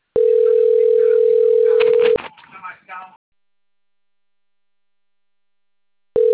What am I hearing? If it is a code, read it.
Ringback tone (North America), 440 + 480 Hz.